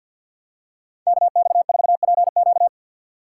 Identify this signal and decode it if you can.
Morse code, heard as a keyed tone, about 52 wpm, 700 Hz. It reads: KX4PQ
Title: K1USN SST QSO Practice - 50wpm